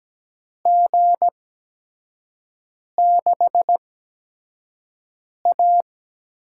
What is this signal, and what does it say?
Morse code, heard as a keyed tone, about 17 wpm, spaced as for 5 wpm, 700 Hz. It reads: G6A
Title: USA Call Signs - 4 Characters - 8x Character Spacing - 17wpm